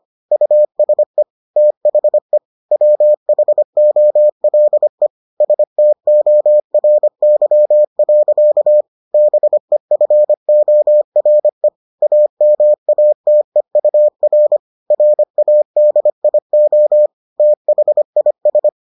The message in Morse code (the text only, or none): USE THE WHOLE STORY. BEFORE AMATEUR RADIO THIS